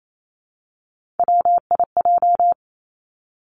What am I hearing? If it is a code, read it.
Morse code, heard as a keyed tone, about 28 wpm, 700 Hz. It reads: WIJ
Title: Three Challenging Letter-Number Combinations 28wpm